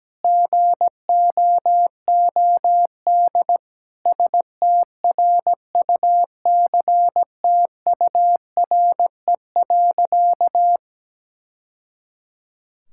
Morse code, heard as a keyed tone, about 17 wpm, 700 Hz. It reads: GOOD STRUCTURE.
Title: Sentences from Top 1000 Words - Review Part 6 - 17wpm